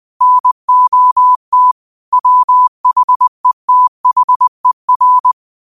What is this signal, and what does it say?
Morse code, heard as a keyed tone, about 20 wpm, 1000 Hz. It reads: NOT WHETHER